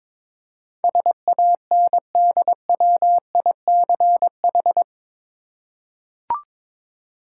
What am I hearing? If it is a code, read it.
Morse code, heard as a keyed tone, about 22 wpm, 700 Hz. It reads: SANDWICH
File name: practice / 22wpm / Eight Letter Words - 22wpm